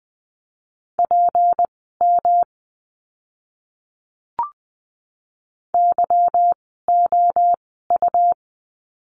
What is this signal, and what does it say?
Morse code, heard as a keyed tone, about 20 wpm, spaced as for 14 wpm, 700 Hz. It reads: PM YOU
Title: 153 Lesson 25f - Letter p - Words 20wpm